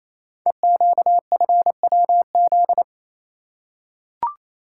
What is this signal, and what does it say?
Morse code, heard as a keyed tone, about 29 wpm, 690 Hz. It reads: EQFWZ
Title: Code Groups - 5 Letter:Numbers - 28wpm